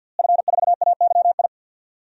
Morse code, heard as a keyed tone, about 52 wpm, 700 Hz. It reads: F4AYI
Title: DXCC-100 Call Signs - 5 Characters - Repeated 2x - 50wpm